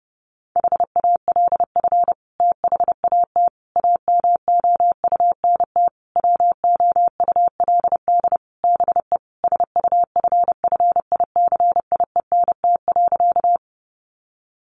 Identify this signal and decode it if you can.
Morse code, heard as a keyed tone, about 30 wpm, 700 Hz. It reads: HALF THAT AMOUNT WOULD BE SUFFICIENT.
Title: Sentences from Top 1400 Words 30wpm